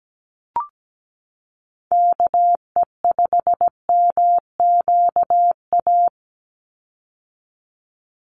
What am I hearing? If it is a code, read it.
Morse code, heard as a keyed tone, about 17 wpm, 700 Hz. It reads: KE5MQA
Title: USA Call Signs 17wpm